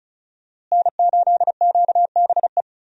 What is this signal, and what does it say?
Morse code, heard as a keyed tone, about 34 wpm, 700 Hz. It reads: N8QBE
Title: USA Call Signs - 5 Characters - Repeated 2x - 35wpm